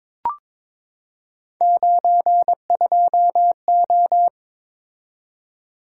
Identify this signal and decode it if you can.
Morse code, heard as a keyed tone, about 21 wpm, 700 Hz. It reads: 92O
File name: practice / 22wpm / Three Letter-Number Combinations - 22wpm